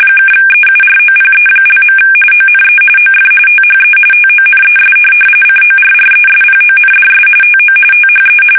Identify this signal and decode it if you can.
RTTY, about 45.45 baud. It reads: (EUROPE). ONE OF THE TONES CORRESPONDS TO THE
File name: fsk850 / teletype-RTTY